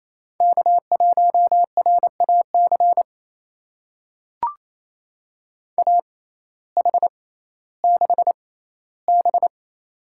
Morse code, heard as a keyed tone, about 28 wpm, spaced as for 15 wpm, 700 Hz. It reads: K1RAC AH6B
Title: USA Call Signs - 6x Character Spacing - 28wpm